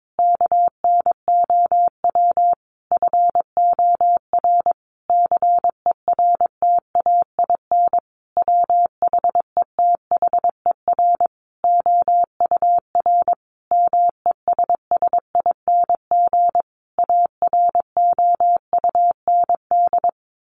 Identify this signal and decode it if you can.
Morse code, heard as a keyed tone, about 22 wpm, 700 Hz. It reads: KNOW FOR CERTAIN WHETHER OUR MESSING AROUND
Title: A Sound Of Thunder 22wpm